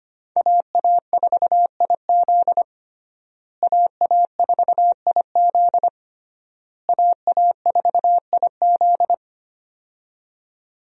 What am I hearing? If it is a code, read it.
Morse code, heard as a keyed tone, about 25 wpm, 700 Hz. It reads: AA4IZ AA4IZ AA4IZ
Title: USA Call Signs - 5 Characters - Repeated 3x - 25wpm